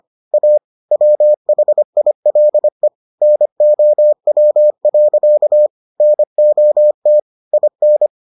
Morse code, heard as a keyed tone, about 25 wpm, 600 Hz. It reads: A WHILE NOW. NOT IN